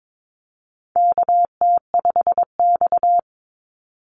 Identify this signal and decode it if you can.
Morse code, heard as a keyed tone, about 22 wpm, 700 Hz. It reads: KT5X
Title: NA SOTA Chaser Callsigns 22wpm